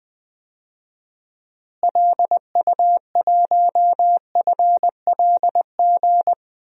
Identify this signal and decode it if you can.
Morse code, heard as a keyed tone, about 21 wpm, 700 Hz. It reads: LU1FLG
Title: DXCC-100 Call Signs - 6 Characters - Repeated 3x - 20wpm